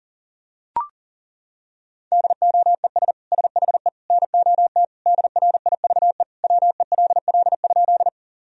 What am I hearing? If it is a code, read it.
Morse code, heard as a keyed tone, about 40 wpm, 700 Hz. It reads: DOES SHE NOT DRIVE WELL?
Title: Sentences from Top 1000 Words - Review Part 2 - 40wpm